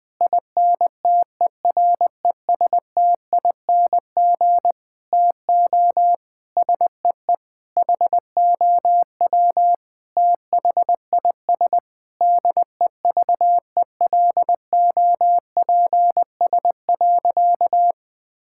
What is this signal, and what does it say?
Morse code, heard as a keyed tone, about 20 wpm, 700 Hz. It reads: INTERESTING TO SEE HOW THIS DEVELOPS.
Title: Sentences with 7-Word Phrases - 20wpm